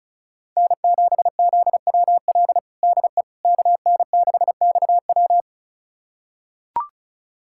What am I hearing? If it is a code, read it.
Morse code, heard as a keyed tone, about 34 wpm, 700 Hz. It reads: N7ZWL DE KN6XW
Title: USA Callsign DE Callsign - 5 Characters - 35wpm